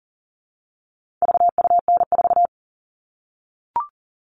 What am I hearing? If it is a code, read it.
Morse code, heard as a keyed tone, about 39 wpm, 700 Hz. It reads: VUN4
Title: All Cut Numbers - 4-Digit 40wpm